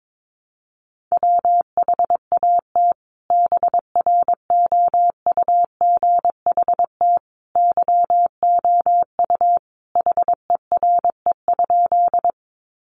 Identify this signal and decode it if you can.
Morse code, heard as a keyed tone, about 22 wpm, 700 Hz. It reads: WHAT BROUGHT YOU HERE?
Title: Sentences from Top 500 Words 22wpm